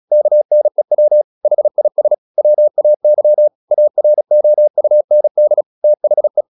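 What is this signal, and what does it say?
Morse code, heard as a keyed tone, about 36 wpm, 600 Hz. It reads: KNEW HIS WAY AROUND THE
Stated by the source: Morse practice at 36 wpm, text of Ken Follett's novel Never, from CW med Gnister.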